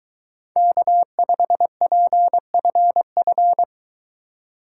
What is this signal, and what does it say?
Morse code, heard as a keyed tone, about 23 wpm, 700 Hz. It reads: K5PFF